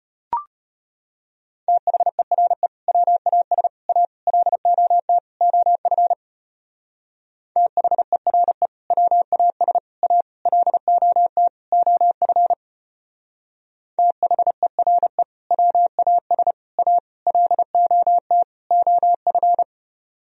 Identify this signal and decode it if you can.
Morse code, heard as a keyed tone, about 30 wpm, 700 Hz. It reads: THERE WAS A LOT OF THERE WAS A LOT OF THERE WAS A LOT OF
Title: Speed-Racing - Sets of 5 Words 25wpm